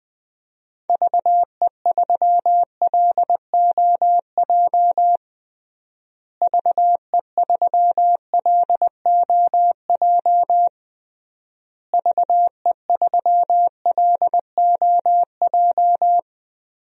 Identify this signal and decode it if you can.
Morse code, heard as a keyed tone, about 20 wpm, 700 Hz. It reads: VE3LOJ VE3LOJ VE3LOJ
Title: DXCC-100 Call Signs - 6 Characters - Repeated 3x - 20wpm